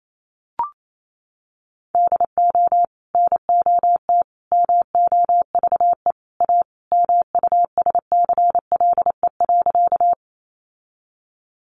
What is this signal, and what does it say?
Morse code, heard as a keyed tone, about 28 wpm, 700 Hz. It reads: DO NOT MOVE A MUSCLE.
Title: Sentences from Top 1300 Words 28wpm